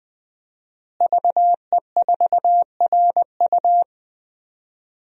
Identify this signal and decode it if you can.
Morse code, heard as a keyed tone, about 20 wpm, 700 Hz. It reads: VE4RU
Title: DXCC-100 Call Signs - 5 Characters - Repeated 3x - 20wpm